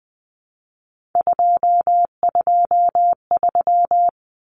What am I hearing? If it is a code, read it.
Morse code, heard as a keyed tone, about 20 wpm, 700 Hz. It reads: 223
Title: Common Cut Numbers - 3-Digit 20wpm